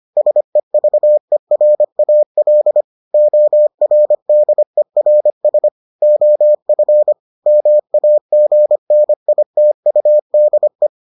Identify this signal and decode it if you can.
Morse code, heard as a keyed tone, about 25 wpm, 600 Hz. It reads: SEVERAL ORDERS OF MAGNITUDE